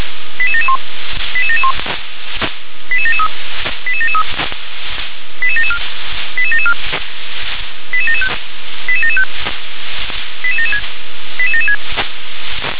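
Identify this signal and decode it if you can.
Five-tone selective call (ZVEI), heard as quick runs of five tones, tones of about 70 ms. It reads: 88771 88771 88773 88773 88774 88774 88775 88775 88776 88776
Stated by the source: ZVEI five-tone selective call received on SDR.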